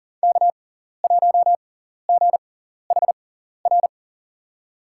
Morse code, heard as a keyed tone, about 40 wpm, spaced as for 14 wpm, 700 Hz. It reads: K1GHR
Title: USA Call Signs - 5 Characters - 6x Character Spacing - 40wpm